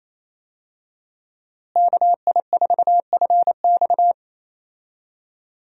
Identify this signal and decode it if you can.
Morse code, heard as a keyed tone, about 28 wpm, 700 Hz. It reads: KI4FX
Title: USA Call Signs - 5 Characters - Repeated 3x - 28wpm